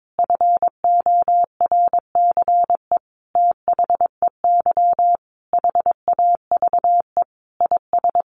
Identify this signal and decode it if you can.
Morse code, heard as a keyed tone, about 22 wpm, 700 Hz. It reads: FORCE THEY HAVE IS
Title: Sentences from Top 1500 Words 22wpm